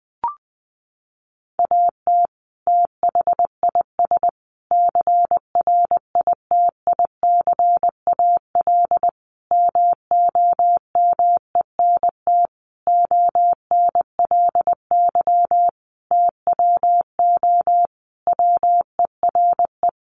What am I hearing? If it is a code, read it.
Morse code, heard as a keyed tone, about 20 wpm, 700 Hz. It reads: AT THIS CRITICAL MOMENT ONLY TWO WERE
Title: Sentences from Top 1100 Words 20wpm